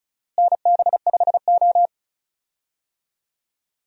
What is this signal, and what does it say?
Morse code, heard as a keyed tone, about 36 wpm, 700 Hz. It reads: NB5O
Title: USA Call Signs - 4 Characters - Repeated 2x - 35wpm